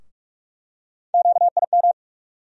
Morse code, heard as a keyed tone, about 46 wpm, 700 Hz. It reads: QIM
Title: USA Call Sign Postfixes - 3 Letters 45wpm